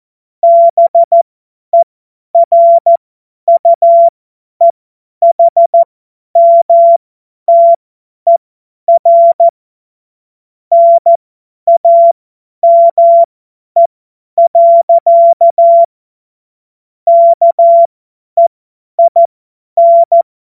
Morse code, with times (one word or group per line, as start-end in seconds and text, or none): 0.4-9.5 BERUEHMTER
10.7-15.8 NAME.
17.1-20.2 KEIN